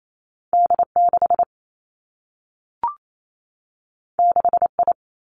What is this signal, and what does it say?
Morse code, heard as a keyed tone, about 28 wpm, 700 Hz. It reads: D 6 6 I
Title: Two Challenging Letter-Number Combinations 28wpm